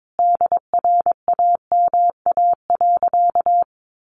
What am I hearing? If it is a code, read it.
Morse code, heard as a keyed tone, about 22 wpm, 700 Hz. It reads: DRAMA.
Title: Sentences from Top 1500 Words 22wpm